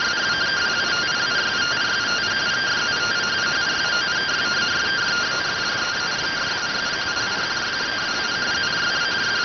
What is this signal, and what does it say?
Two-tone teleprinter signal, about 170 Hz shift.